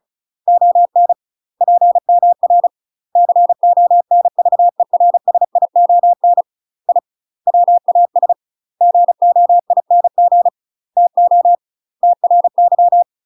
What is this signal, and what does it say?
Morse code, heard as a keyed tone, about 35 wpm, 700 Hz. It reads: ON PMR CONVERSION I WAS GOING TO TRY
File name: UK MASTER QSOs TRAINING / 35WPM / UK MASTER QSOs 0043